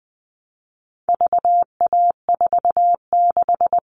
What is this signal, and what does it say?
Morse code, heard as a keyed tone, about 20 wpm, 700 Hz. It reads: VA46
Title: All Cut Numbers - 4-Digit 20wpm